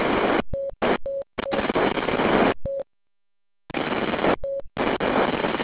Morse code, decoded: none